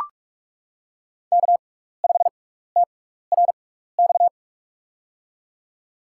Morse code, heard as a keyed tone, about 44 wpm, spaced as for 15 wpm, 700 Hz. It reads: K5TRX
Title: USA Call Signs - 5 Characters - 6x Character Spacing - 45wpm